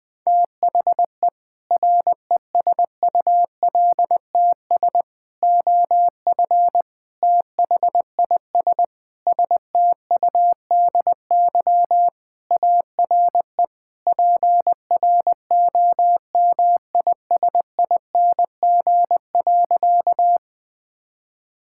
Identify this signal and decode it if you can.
Morse code, heard as a keyed tone, about 19 wpm, 700 Hz. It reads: THE RESULTS OF THIS STUDY ARE PROMISING.
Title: Sentences with 5-Word Phrases - 20wpm